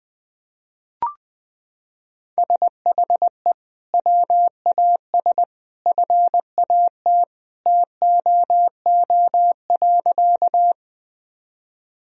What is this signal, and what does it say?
Morse code, heard as a keyed tone, about 20 wpm, 700 Hz. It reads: SHE WAS FAT TOO.
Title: Sentences from Top 1000 Words - Review Part 4 - 20wpm